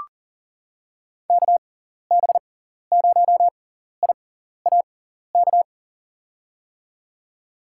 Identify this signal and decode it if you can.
Morse code, heard as a keyed tone, about 38 wpm, spaced as for 14 wpm, 700 Hz. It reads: KB0IAK
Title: USA Call Signs - 6x Character Spacing - 40wpm